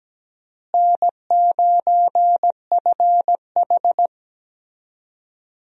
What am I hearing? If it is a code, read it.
Morse code, heard as a keyed tone, about 17 wpm, 700 Hz. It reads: N9FH
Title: USA Call Signs - 4 Characters - Repeated 3x - 17wpm